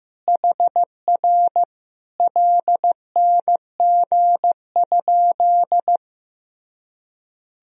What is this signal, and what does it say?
Morse code, heard as a keyed tone, about 16 wpm, 700 Hz. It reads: HR LNG?
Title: Sentences from Top 200 Words and Abbreviations 15wpm